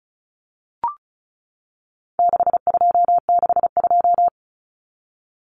Morse code, heard as a keyed tone, about 34 wpm, 700 Hz. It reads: 6262
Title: Numbers - 4-Digit 35wpm